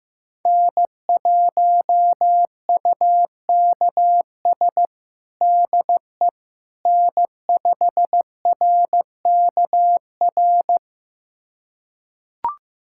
Morse code, heard as a keyed tone, about 16 wpm, 700 Hz. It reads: N1UKS DE N5RKR
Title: USA Callsign DE Callsign - 5 Characters - 15wpm